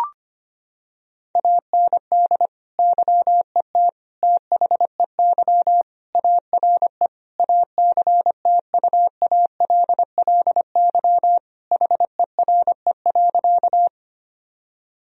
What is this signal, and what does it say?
Morse code, heard as a keyed tone, about 25 wpm, 700 Hz. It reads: AND YET THEY ARE ACTUALLY HERE.
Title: Sentences from Top 1000 Words - Review Part 4 - 25wpm